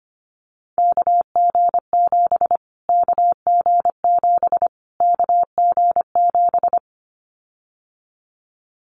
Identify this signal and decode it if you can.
Morse code, heard as a keyed tone, about 25 wpm, 700 Hz. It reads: KG7 KG7 KG7
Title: Callsign Prefixes for Top 25 DXCC - Repeated 3x - 25wpm